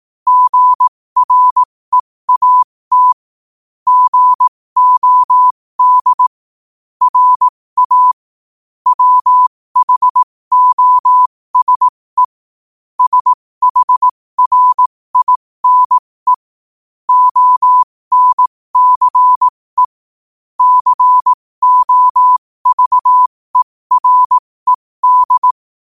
Morse code, heard as a keyed tone, about 18 wpm, spaced as for 15 wpm, 1000 Hz. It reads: GREAT GOD RA WHOSE SHRINE ONCE COVERED
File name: qotdfiles / 15wpm